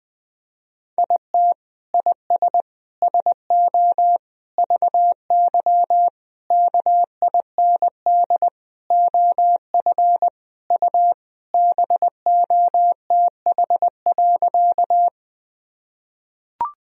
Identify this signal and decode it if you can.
Morse code, heard as a keyed tone, about 21 wpm, 700 Hz. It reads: IT IS SO VY KIND OF U BOTH.
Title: Sentences from Top 300 Words and Abbreviations 20wpm